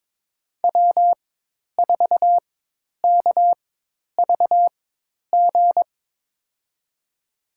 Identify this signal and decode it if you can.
Morse code, heard as a keyed tone, about 23 wpm, spaced as for 10 wpm, 700 Hz. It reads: W4KVG
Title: USA Call Signs - 4x Character Spacing - 22wpm